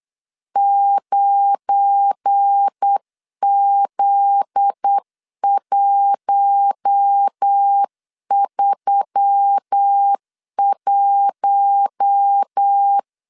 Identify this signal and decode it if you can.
Morse code, heard as a keyed tone, about 8 wpm, 800 Hz. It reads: 9Z131
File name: CW / DET / 1 9 1500C S A 8wpm 5637